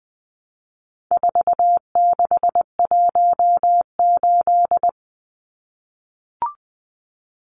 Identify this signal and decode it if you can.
Morse code, heard as a keyed tone, about 20 wpm, 700 Hz. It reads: 4618
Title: Numbers - 4-Digit 20wpm